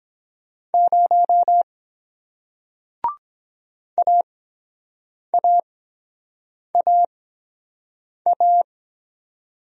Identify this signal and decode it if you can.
Morse code, heard as a keyed tone, about 23 wpm, spaced as for 7 wpm, 700 Hz. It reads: 0 AAAA